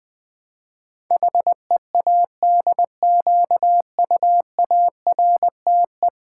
Morse code, heard as a keyed tone, about 20 wpm, 700 Hz. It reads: HEADQUARTE
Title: Build A Word - 5 Letters At A Time - 12-Letter Words - 20wpm